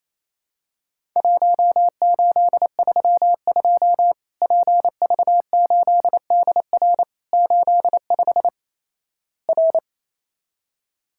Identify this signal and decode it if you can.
Morse code, heard as a keyed tone, about 28 wpm, 700 Hz. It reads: 1832 PV8DR 85 R
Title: Worked All Europe DX Contest - QTC Exchange - 28wpm